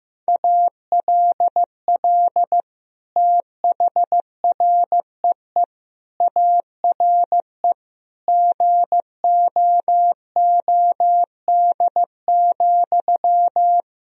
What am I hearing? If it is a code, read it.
Morse code, heard as a keyed tone, about 15 wpm, 700 Hz. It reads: ALL THREE ARE GOOD,
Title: Sentences from Top 1000 Words - Review Part 4 - 15wpm